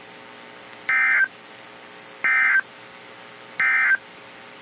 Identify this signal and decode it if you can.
Emergency Alert System (SAME) tones, heard as harsh buzzing bursts. It is heard from 0.8 to 4.1 s.
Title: NOAA Weather Radio - KEC55